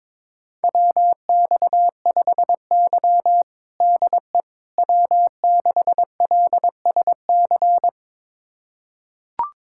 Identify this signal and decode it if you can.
Morse code, heard as a keyed tone, about 22 wpm, 700 Hz. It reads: WX5Y DE W6LSC
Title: USA Callsign DE Callsign 22wpm